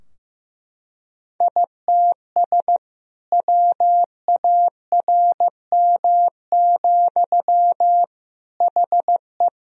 Morse code, heard as a keyed tone, about 15 wpm, 700 Hz. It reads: ITS WARM, HE
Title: Sentences from Top 1000 Words - Review Part 2 - 15wpm